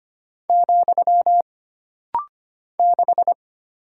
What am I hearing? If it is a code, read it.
Morse code, heard as a keyed tone, about 25 wpm, 700 Hz. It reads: , 6
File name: Single Character (Letter:Number:Punctuation:Prosign) - Rapid-Fire - 25wpm